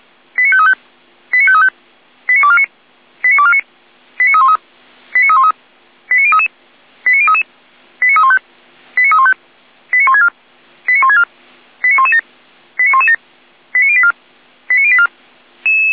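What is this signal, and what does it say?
Five-tone selective call (ZVEI), heard as quick runs of five tones, tones of about 70 ms. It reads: 89535 89535 89249 89249 89313 89313 89033 89033 89315 89315 89164 89164 89117 89117 89084 89084
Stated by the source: ZVEI five-tone selective call received with an SDR.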